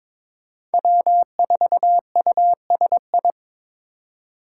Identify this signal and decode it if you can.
Morse code, heard as a keyed tone, about 21 wpm, 700 Hz. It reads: W4USI
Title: K1USN SST QSO Practice - 22wpm